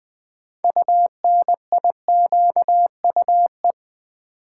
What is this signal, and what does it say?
Morse code, heard as a keyed tone, about 20 wpm, 700 Hz. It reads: UNIQUE
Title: Sentences from Top 1000 Words - Review Part 7 - New Word Drill - 20wpm